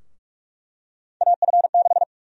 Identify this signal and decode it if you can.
Morse code, heard as a keyed tone, about 48 wpm, 700 Hz. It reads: AF6
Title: USA Call Sign Prefixes - Repeated 2x - 45wpm